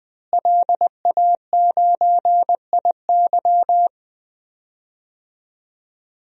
Morse code, heard as a keyed tone, about 20 wpm, 700 Hz. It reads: LA9IY
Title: DXCC-100 Call Signs - 5 Characters - Repeated 2x - 20wpm